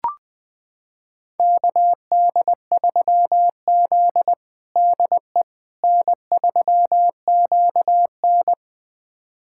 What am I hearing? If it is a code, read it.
Morse code, heard as a keyed tone, about 21 wpm, 700 Hz. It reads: KD3Z DE N3QN
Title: USA Callsign DE Callsign - 4 Characters - 20wpm